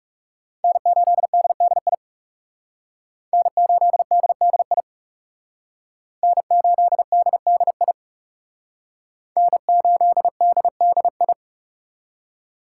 Morse code, heard as a keyed tone, about 40 wpm, 700 Hz. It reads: N8DDI N8DDI N8DDI N8DDI